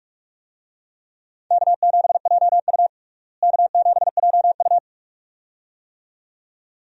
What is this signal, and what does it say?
Morse code, heard as a keyed tone, about 47 wpm, 700 Hz. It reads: K7JU K7JU
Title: USA Call Signs - 4 Characters - Repeated 2x - 45wpm